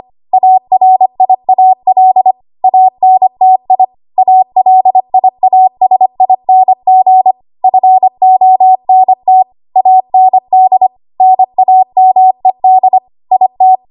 Morse code, heard as a keyed tone, about 25 wpm, 750 Hz. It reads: ARIAL ANTI ALIASING FONT AND NAMED IT
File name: Archive / 25 WPM / 25WPM